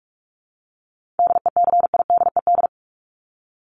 Morse code, heard as a keyed tone, about 44 wpm, 700 Hz. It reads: DECIDED
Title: ED Suffix - Top 100 Words with Base Words 45wpm